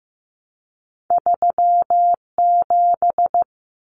Morse code, heard as a keyed tone, about 15 wpm, 700 Hz. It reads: 37
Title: Numbers - 2-Digit 15wpm